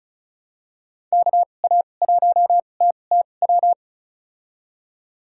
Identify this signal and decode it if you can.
Morse code, heard as a keyed tone, about 34 wpm, spaced as for 25 wpm, 700 Hz. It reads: KA1TTW
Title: USA Call Signs - 6 Characters - 2x Character Spacing - 35wpm